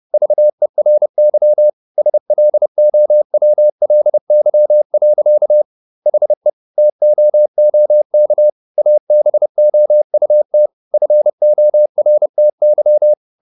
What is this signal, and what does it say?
Morse code, heard as a keyed tone, about 30 wpm, 600 Hz. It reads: VERY SLOWLY. HE TOOK ABOUT FORTY